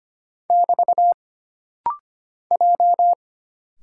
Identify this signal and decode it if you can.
Morse code, heard as a keyed tone, about 25 wpm, 700 Hz. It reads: <BT> J
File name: Single Character (Letter:Number:Punctuation:Prosign) - Rapid-Fire - 25wpm